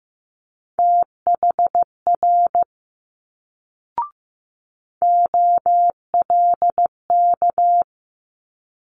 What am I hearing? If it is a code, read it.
Morse code, heard as a keyed tone, about 15 wpm, 700 Hz. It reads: THR OLK
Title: Multi-letter Phonemes Quick Fire 15wpm